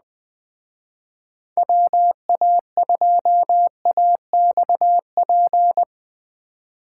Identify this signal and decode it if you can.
Morse code, heard as a keyed tone, about 20 wpm, 700 Hz. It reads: WA2AXP